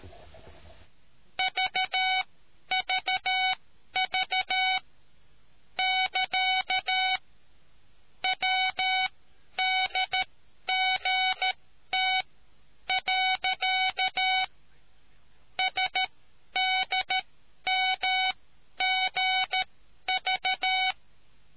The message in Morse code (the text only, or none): VVV <KA> WDGT. SDMGV